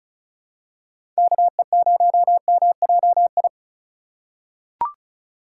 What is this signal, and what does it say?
Morse code, heard as a keyed tone, about 34 wpm, 690 Hz. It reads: KE0MJI
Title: USA Call Signs - 6 Characters - 2x Character Spacing - 35wpm